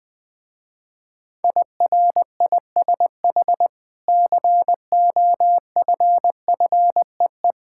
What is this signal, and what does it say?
Morse code, heard as a keyed tone, about 20 wpm, 700 Hz. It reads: IRISH COFFEE
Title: Difficult Words with Hints - 20wpm